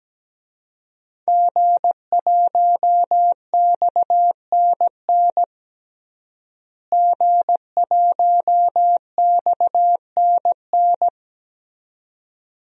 Morse code, heard as a keyed tone, about 17 wpm, 700 Hz. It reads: G1XNN G1XNN
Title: DXCC-100 Call Signs - 5 Characters - Repeated 2x - 17wpm